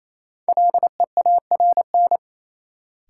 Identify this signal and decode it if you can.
Morse code, heard as a keyed tone, about 28 wpm, 700 Hz. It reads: LEARN
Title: Sentences from Top 300 Words - New Word Drill - 28wpm